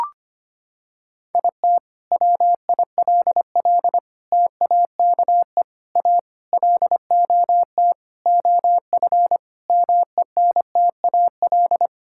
Morse code, heard as a keyed tone, about 25 wpm, 700 Hz. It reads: IT WILL TAKE A LOT OF MENTAL
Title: Sentences from Top 1000 Words - Review Part 6 - 25wpm